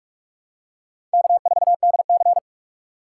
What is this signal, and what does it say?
Morse code, heard as a keyed tone, about 46 wpm, 700 Hz. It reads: K4DC
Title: USA Call Signs - 4 Characters - 4x Character Spacing - 45wpm